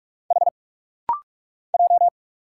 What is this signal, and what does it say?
Morse code, heard as a keyed tone, about 46 wpm, 690 Hz. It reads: H J
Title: Single Letters - Rapid-Fire 45wpm